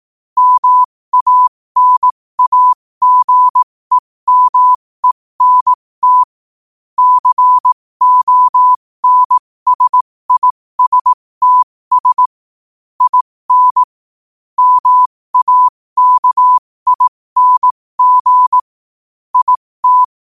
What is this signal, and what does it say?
Morse code, heard as a keyed tone, about 18 wpm, spaced as for 15 wpm, 1000 Hz. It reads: MANAGEMENT CONSISTS IN MAKING IT
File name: qotdfiles / 15wpm